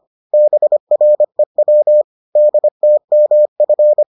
Morse code, heard as a keyed tone, about 25 wpm, 600 Hz. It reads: BREW DTMF